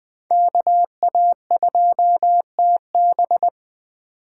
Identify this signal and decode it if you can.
Morse code, heard as a keyed tone, about 20 wpm, 700 Hz. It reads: KA2TB
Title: Code Groups - 5 Letter:Numbers - 20wpm